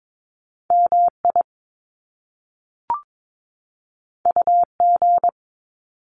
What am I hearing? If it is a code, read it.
Morse code, heard as a keyed tone, about 22 wpm, 700 Hz. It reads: MI UG
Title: Top Two Letter Combinations in Words 22wpm